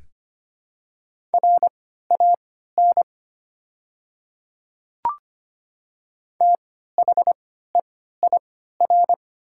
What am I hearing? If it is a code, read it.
Morse code, heard as a keyed tone, about 25 wpm, spaced as for 14 wpm, 690 Hz. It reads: RAN THEIR
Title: 3x Character Spacing - Top 500 Words - 25wpm